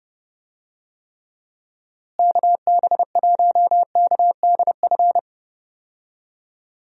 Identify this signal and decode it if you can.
Morse code, heard as a keyed tone, about 30 wpm, 700 Hz. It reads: KB1KDF
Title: USA Call Signs - 6 Characters - Repeated 3x - 30wpm